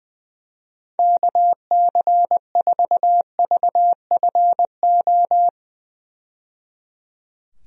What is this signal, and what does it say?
Morse code, heard as a keyed tone, about 20 wpm, 700 Hz. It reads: KC4VFO
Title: USA Call Signs - 6 Characters - 20wpm